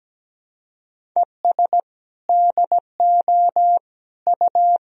Morse code, heard as a keyed tone, about 17 wpm, 700 Hz. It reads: ES DO U
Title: Sentences from Top 200 Words and Abbreviations 17wpm